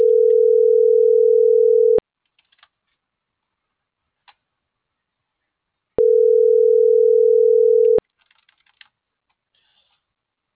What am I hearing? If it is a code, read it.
Ringback tone (North America), 440 + 480 Hz.